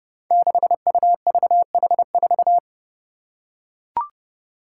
Morse code, heard as a keyed tone, about 31 wpm, 700 Hz. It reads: 6UVH4
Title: Five Challenging Letter-Number Combinations 30wpm